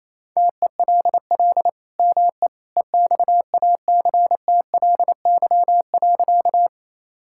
Morse code, heard as a keyed tone, about 28 wpm, 700 Hz. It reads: TELL ME EXACTLY.
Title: Sentences from Top 1000 Words - Review Part 6 - 28wpm